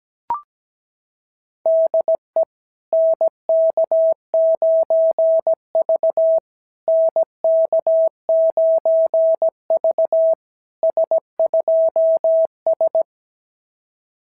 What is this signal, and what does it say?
Morse code, heard as a keyed tone, about 17 wpm, 660 Hz. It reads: DE NK9V NK9V S2S
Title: SOTA QSO Practice 17wpm